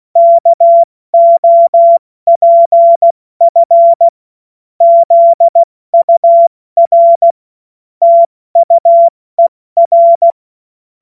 Morse code, heard as a keyed tone, about 16 wpm, 680 Hz. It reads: KOPF ZUR TUER
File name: Morsekurs / Witze / Witz 13 Februar 2024 70 F80